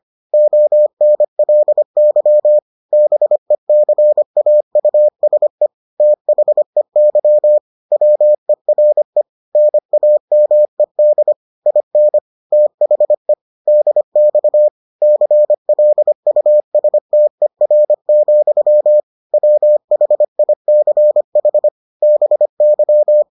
Morse code, heard as a keyed tone, about 25 wpm, 600 Hz. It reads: ONLY BECAUSE THEY WERE NAMED IN THE DX CLUSTER, WHICH BY